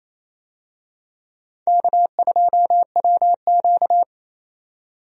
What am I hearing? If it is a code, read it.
Morse code, heard as a keyed tone, about 28 wpm, 700 Hz. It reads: K2WQ
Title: USA Call Signs - 4 Characters - 28wpm